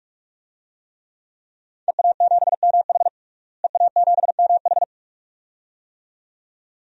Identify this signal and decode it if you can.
Morse code, heard as a keyed tone, about 41 wpm, 700 Hz. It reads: EA7MH EA7MH
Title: DXCC-100 Call Signs - 5 Characters - Repeated 2x - 45wpm